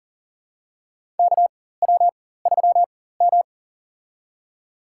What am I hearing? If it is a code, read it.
Morse code, heard as a keyed tone, about 37 wpm, spaced as for 18 wpm, 700 Hz. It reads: KW3M
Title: USA Call Signs - 4 Characters - 4x Character Spacing - 40wpm